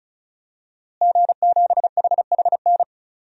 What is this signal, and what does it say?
Morse code, heard as a keyed tone, about 34 wpm, 700 Hz. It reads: G7HHN
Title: DXCC-100 Call Signs - 5 Characters - Repeated 2x - 35wpm